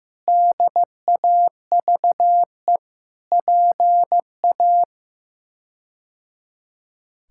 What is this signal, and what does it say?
Morse code, heard as a keyed tone, about 16 wpm, 700 Hz. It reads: DAVE PA
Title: K1USN SST QSO Practice - 15wpm